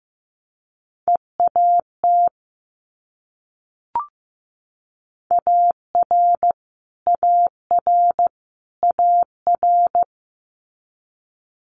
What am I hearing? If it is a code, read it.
Morse code, heard as a keyed tone, about 15 wpm, 700 Hz. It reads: EAT AR AR AR
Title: Multi-letter Phonemes Repeated 3x 15wpm